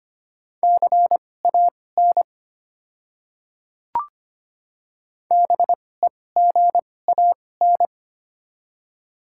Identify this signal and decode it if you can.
Morse code, heard as a keyed tone, about 25 wpm, spaced as for 18 wpm, 700 Hz. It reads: CAN BEGAN
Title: 2x Character Spacing - Top 500 Words - 25wpm